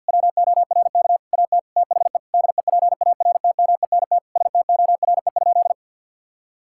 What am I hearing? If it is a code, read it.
Morse code, heard as a keyed tone, about 52 wpm, 700 Hz. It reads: WORK AT THE DEPARTMENT STORE?
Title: Sentences from Top 1000 Words - Review Part 5 - 50wpm